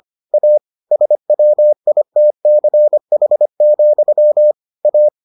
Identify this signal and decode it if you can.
Morse code, heard as a keyed tone, about 25 wpm, 600 Hz. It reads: A SWITCH, A